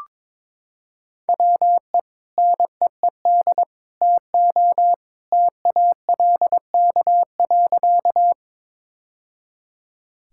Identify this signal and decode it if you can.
Morse code, heard as a keyed tone, about 21 wpm, 700 Hz. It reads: WE NEED TO TALK.
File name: practice / 22wpm / Sentences with 3-Word Phrases - 22wpm